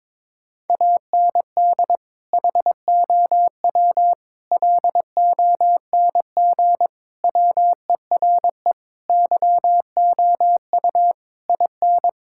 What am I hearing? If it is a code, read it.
Morse code, heard as a keyed tone, about 22 wpm, 700 Hz. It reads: AND HOW LONG WERE YOU IN
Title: Sentences from Top 1600 Words 22wpm